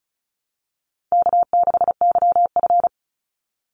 Morse code, heard as a keyed tone, about 34 wpm, 700 Hz. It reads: K6YF
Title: Easy Call Signs Repeated 2x - 35wpm